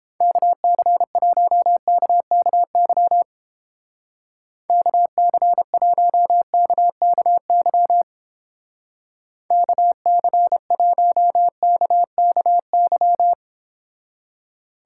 Morse code, heard as a keyed tone, about 29 wpm, 700 Hz. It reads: KC1KKY KC1KKY KC1KKY